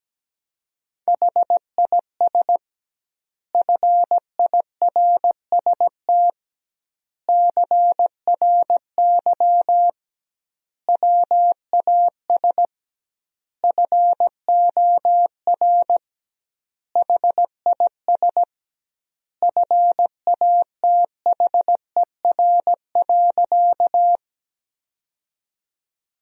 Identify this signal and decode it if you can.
Morse code, heard as a keyed tone, about 17 wpm, 700 Hz. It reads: HIS FIRST CRY WAS FOR HIS FATHER.
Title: 2x Word-Spacing - Sentences from Top 500 Words - 17wpm